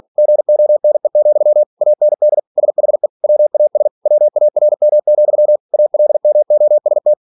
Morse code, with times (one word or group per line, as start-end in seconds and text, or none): 0.2-1.6 GONE,
1.8-2.4 AND
2.6-3.1 SHE
3.2-3.9 WAS
4.1-5.6 WARM,
5.7-7.1 ALMOST